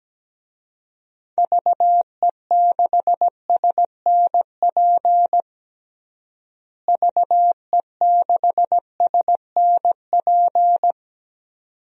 Morse code, heard as a keyed tone, about 17 wpm, 700 Hz. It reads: VE6SNP VE6SNP
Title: DXCC-100 Call Signs - 6 Characters - Repeated 2x - 17wpm